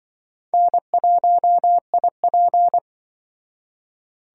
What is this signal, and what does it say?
Morse code, heard as a keyed tone, about 23 wpm, 700 Hz. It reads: N1IP
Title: Speed Racing - USA Call Signs 20wpm